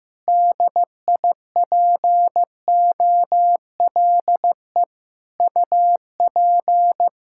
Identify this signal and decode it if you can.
Morse code, heard as a keyed tone, about 15 wpm, 700 Hz. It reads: DIPOLE UP
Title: Practice QSO - Exchange 2 - Info - 15wpm